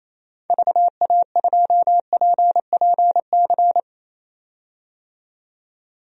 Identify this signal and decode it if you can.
Morse code, heard as a keyed tone, about 28 wpm, 700 Hz. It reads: VA2PPC
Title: DXCC-100 Call Signs - 6 Characters - 28wpm